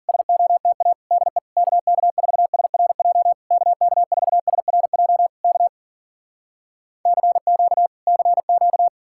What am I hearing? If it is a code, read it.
Morse code, heard as a keyed tone, about 53 wpm, 700 Hz. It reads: SOTA DE KK4SRJ KK4SRJ K CQ CQ